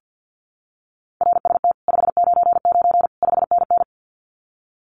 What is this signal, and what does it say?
Morse code, heard as a keyed tone, about 50 wpm, 700 Hz. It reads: RST 599 5NN
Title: Common QSO Phrases Encore 50wpm